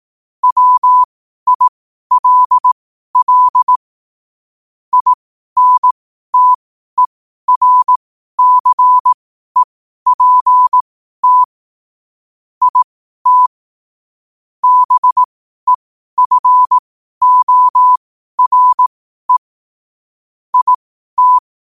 Morse code, heard as a keyed tone, about 18 wpm, spaced as for 12 wpm, 1000 Hz. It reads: WILL INTERCEPT IT BEFORE IT
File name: qotdfiles / 12wpm